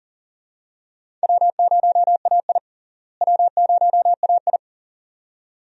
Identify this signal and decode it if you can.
Morse code, heard as a keyed tone, about 37 wpm, 700 Hz. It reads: W0AI W0AI
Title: USA Call Signs - 4 Characters - Repeated 2x - 40wpm